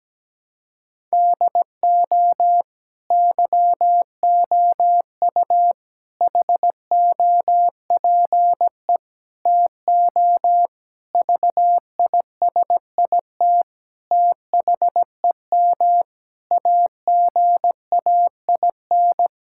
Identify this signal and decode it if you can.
Morse code, heard as a keyed tone, about 17 wpm, 700 Hz. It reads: DO YOU HOPE TO VISIT THEM AGAIN
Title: Sentences from Top 1000 Words - Review Part 6 - 17wpm